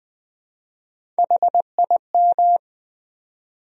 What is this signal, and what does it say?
Morse code, heard as a keyed tone, about 20 wpm, 700 Hz. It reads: HIM
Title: Top 100 Words 20wpm